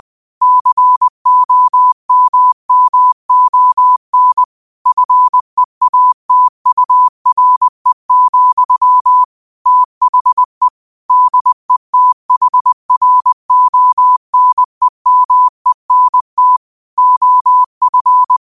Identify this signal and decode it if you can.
Morse code, heard as a keyed tone, about 20 wpm, 1000 Hz. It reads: COMMON FEATURE, THE DETHRONEMENT OF